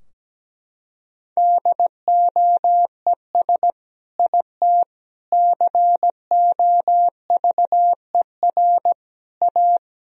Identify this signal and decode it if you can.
Morse code, heard as a keyed tone, about 17 wpm, 700 Hz. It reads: DOES IT COVER A
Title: Sentences from Top 1000 Words - Review Part 2 - 17wpm